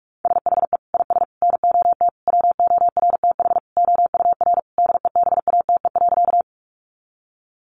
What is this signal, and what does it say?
Morse code, heard as a keyed tone, about 44 wpm, 700 Hz. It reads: SHE IS NOT WORTH OUR DEBATE.
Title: Sentences from Top 1100 Words 45wpm